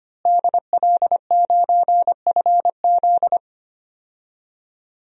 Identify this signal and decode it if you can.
Morse code, heard as a keyed tone, about 25 wpm, 700 Hz. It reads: DL9FZ
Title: DXCC-100 Call Signs - 5 Characters - Repeated 3x - 25wpm